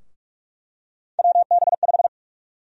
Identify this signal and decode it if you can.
Morse code, heard as a keyed tone, about 47 wpm, 700 Hz. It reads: WB5
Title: USA Call Sign Prefixes - 45wpm